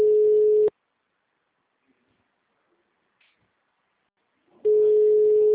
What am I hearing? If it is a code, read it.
Ringback tone (Europe, 425 Hz).